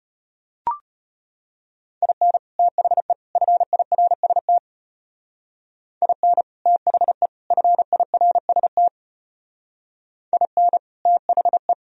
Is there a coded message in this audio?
Yes: Morse code from 2.0 to 11.7 s, heard as a keyed tone, about 32 wpm, 700 Hz. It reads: IN THE FIRST IN THE FIRST IN THE